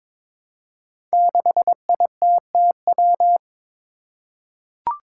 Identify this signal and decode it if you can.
Morse code, heard as a keyed tone, about 22 wpm, 700 Hz. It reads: 6ITTW
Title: Five Challenging Letter-Number Combinations 22wpm